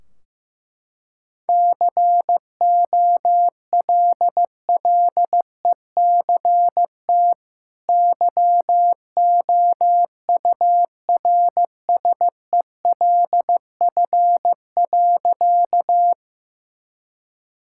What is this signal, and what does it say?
Morse code, heard as a keyed tone, about 14 wpm, 700 Hz. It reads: COLLECT YOURSELF.
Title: Sentences from Top 1600 Words 15wpm